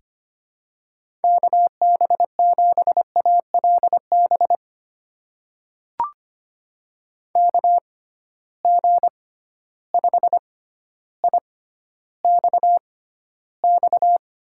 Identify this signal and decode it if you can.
Morse code, heard as a keyed tone, about 25 wpm, spaced as for 13 wpm, 700 Hz. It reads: KB7ALB KG5IXX
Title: USA Call Signs - 6x Character Spacing - 25wpm